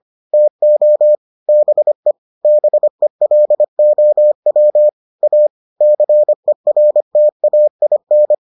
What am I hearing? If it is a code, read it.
Morse code, heard as a keyed tone, about 25 wpm, 600 Hz. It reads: TO BE BELOW A CERTAIN